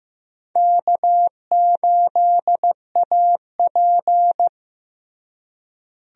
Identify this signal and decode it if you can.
Morse code, heard as a keyed tone, about 15 wpm, 700 Hz. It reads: K8AP
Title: USA Call Signs - 4 Characters - Repeated 2x - 15wpm